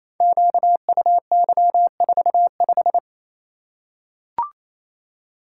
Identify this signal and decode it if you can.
Morse code, heard as a keyed tone, about 27 wpm, 700 Hz. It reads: QUY45
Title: Code Groups - 5 Letter:Numbers - 28wpm